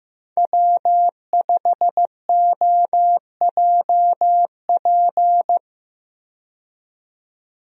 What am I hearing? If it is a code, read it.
Morse code, heard as a keyed tone, about 16 wpm, 700 Hz. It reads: W5OJP
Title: USA Call Signs - 5 Characters - 15wpm